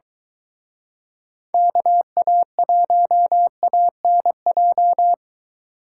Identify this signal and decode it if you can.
Morse code, heard as a keyed tone, about 22 wpm, 700 Hz. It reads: KA1ANJ